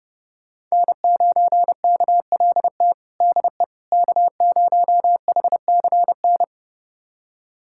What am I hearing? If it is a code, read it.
Morse code, heard as a keyed tone, about 30 wpm, 700 Hz. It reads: N9KLT DE K0HCN
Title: USA Callsign DE Callsign - 5 Characters - 30wpm